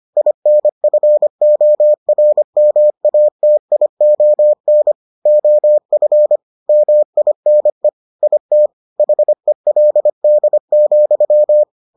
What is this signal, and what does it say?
Morse code, heard as a keyed tone, about 25 wpm, 600 Hz. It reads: INFORMATION OF MINE IT HELD,